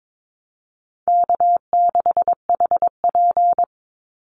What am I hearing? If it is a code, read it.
Morse code, heard as a keyed tone, about 22 wpm, 700 Hz. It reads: K6HP
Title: Easy Call Signs - 22wpm